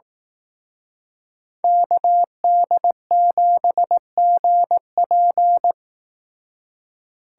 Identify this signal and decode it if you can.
Morse code, heard as a keyed tone, about 18 wpm, 700 Hz. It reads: KD7GP
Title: Speed Racing - USA 2x2 Call Signs 15wpm